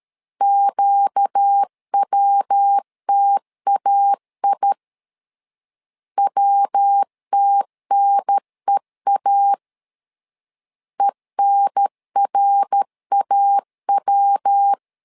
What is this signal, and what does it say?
Morse code, heard as a keyed tone, about 13 wpm, 800 Hz. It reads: QWTAI WTNEA ENRAW
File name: LEZIONI / 4 Q 12wpm 1833